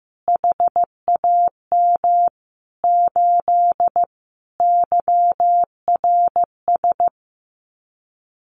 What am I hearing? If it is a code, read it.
Morse code, heard as a keyed tone, about 15 wpm, 700 Hz. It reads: HAM 8 YRS
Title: Common QSO Phrases Encore 15wpm